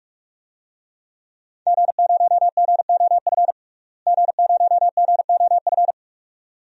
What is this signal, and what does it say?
Morse code, heard as a keyed tone, about 44 wpm, 700 Hz. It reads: G0GOF G0GOF
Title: DXCC-100 Call Signs - 5 Characters - Repeated 2x - 45wpm